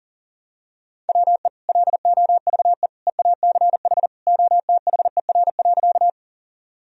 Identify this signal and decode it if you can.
Morse code, heard as a keyed tone, about 40 wpm, 700 Hz. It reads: WE LOVE EACH OTHER.
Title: Sentences from Top 1000 Words - Review Part 1 - 40wpm